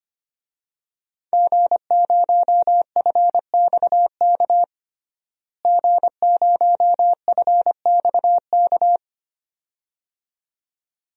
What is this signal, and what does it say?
Morse code, heard as a keyed tone, about 25 wpm, 700 Hz. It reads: G0FXK G0FXK
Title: DXCC-100 Call Signs - 5 Characters - Repeated 2x - 25wpm